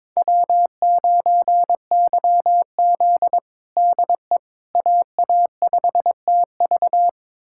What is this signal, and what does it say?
Morse code, heard as a keyed tone, about 22 wpm, 700 Hz. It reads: W9YZ DE AA5TV
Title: USA Callsign DE Callsign 22wpm